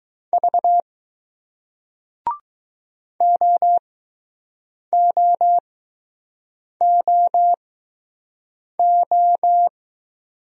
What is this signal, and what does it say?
Morse code, heard as a keyed tone, about 20 wpm, spaced as for 6 wpm, 700 Hz. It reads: V OOOO